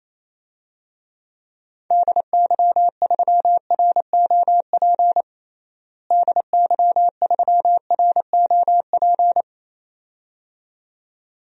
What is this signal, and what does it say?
Morse code, heard as a keyed tone, about 28 wpm, 700 Hz. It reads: DY3ROP DY3ROP
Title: DXCC-100 Call Signs - 6 Characters - Repeated 2x - 28wpm